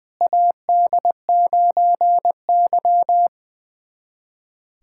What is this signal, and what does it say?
Morse code, heard as a keyed tone, about 20 wpm, 700 Hz. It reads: AD9Y
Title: USA Callsign DE Callsign - 4 Characters - 20wpm